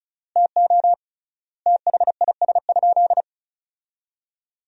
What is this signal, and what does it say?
Morse code, heard as a keyed tone, about 36 wpm, 700 Hz. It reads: TO THIS?
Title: 3x Word-Spacing - Sentences from Top 100 Words - 35wpm